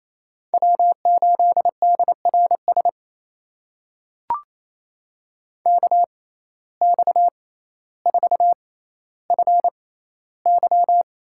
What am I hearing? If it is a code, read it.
Morse code, heard as a keyed tone, about 28 wpm, spaced as for 14 wpm, 700 Hz. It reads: W8DRS KX4FY
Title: USA Call Signs - 5 Characters - 6x Character Spacing - 28wpm